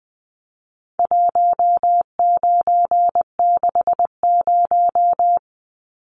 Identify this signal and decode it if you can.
Morse code, heard as a keyed tone, about 20 wpm, 700 Hz. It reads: 1960
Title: Numbers - 4-Digit 20wpm